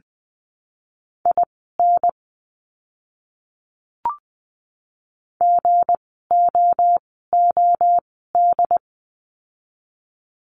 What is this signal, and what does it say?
Morse code, heard as a keyed tone, about 20 wpm, spaced as for 15 wpm, 700 Hz. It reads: IN GOOD
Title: 244 Lesson 39f - Letter z - Words 20wpm